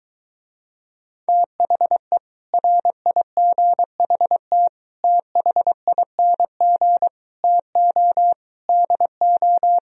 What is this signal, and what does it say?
Morse code, heard as a keyed tone, about 23 wpm, 700 Hz. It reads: THE RIGHT THING TO DO